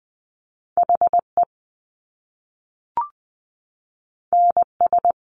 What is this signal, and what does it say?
Morse code, heard as a keyed tone, about 20 wpm, 700 Hz. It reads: HE NS
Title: Top Two Letter Combinations in Words 20wpm